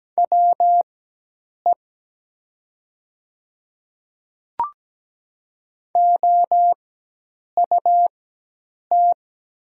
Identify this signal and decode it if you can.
Morse code, heard as a keyed tone, about 17 wpm, spaced as for 8 wpm, 700 Hz. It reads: WE OUT
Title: 4x Character Spacing - Top 100 Words - 17wpm